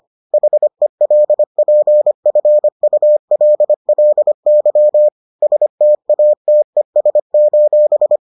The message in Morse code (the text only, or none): HELPFULLY STATES: